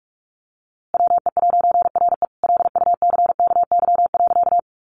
Morse code, heard as a keyed tone, about 44 wpm, 700 Hz. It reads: WE'RE LUCKY.
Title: A Sound Of Thunder 45wpm